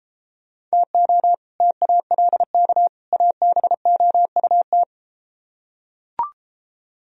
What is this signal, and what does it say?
Morse code, heard as a keyed tone, about 32 wpm, 700 Hz. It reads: TO TALK ABOUT